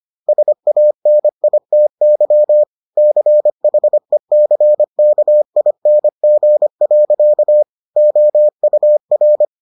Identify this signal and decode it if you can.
Morse code, heard as a keyed tone, about 25 wpm, 600 Hz. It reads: SANITY CHECKING. OUR